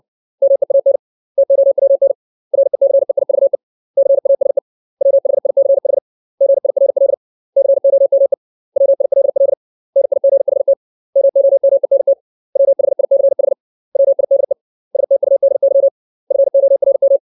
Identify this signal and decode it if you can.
Morse code, heard as a keyed tone, about 60 wpm, 550 Hz. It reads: MEAN TOWN RECEIVE CASE WHICH GIRL COME WILL NIGHT MOMENT WHICH WIDE STUDY FORM
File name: 632 words / 632 Words 60WPM